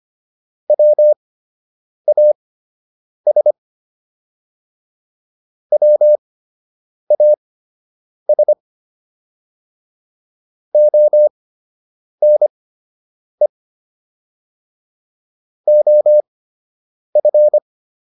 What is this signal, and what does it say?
Morse code, heard as a keyed tone, about 25 wpm, spaced as for 8 wpm, 600 Hz. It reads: WAS WAS ONE OF